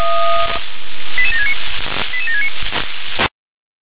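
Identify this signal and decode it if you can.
Five-tone selective call (ZVEI), heard as quick runs of five tones, tones of about 70 ms. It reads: 88769 88769
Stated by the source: ZVEI five-tone selective call received on an SDR.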